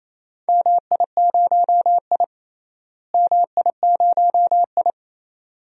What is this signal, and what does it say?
Morse code, heard as a keyed tone, about 28 wpm, 700 Hz. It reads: MI0I MI0I
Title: DXCC-100 Call Signs - 4 Characters - Repeated 2x - 28wpm